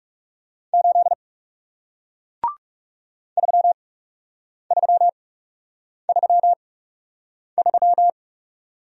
Morse code, heard as a keyed tone, about 37 wpm, spaced as for 8 wpm, 700 Hz. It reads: 8 3333
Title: Speed-Racing Single Letter-Number 30wpm